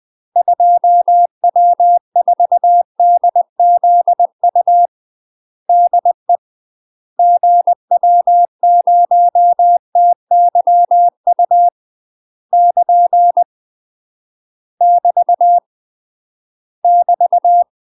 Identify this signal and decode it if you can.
Morse code, heard as a keyed tone, about 20 wpm, 700 Hz. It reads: 2W4DZU DE GW0TYU <KN> <BT> <BT>
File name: UK MASTER QSOs TRAINING / 20WPM / UK MASTER QSOs 0004